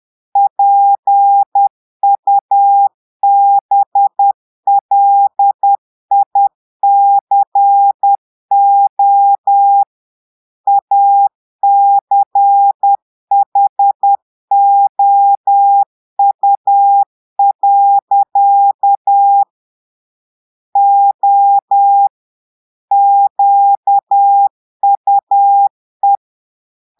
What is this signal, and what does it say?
Morse code, heard as a keyed tone, about 10 wpm, 800 Hz. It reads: PUBLICO ACHOU. O QUE
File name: portugais texte en clair 2 vitesse 10 mots minutes